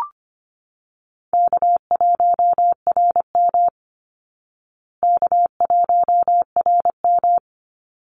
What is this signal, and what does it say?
Morse code, heard as a keyed tone, about 25 wpm, 700 Hz. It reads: K1RM K1RM
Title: Easy Call Signs Repeated 2x - 25wpm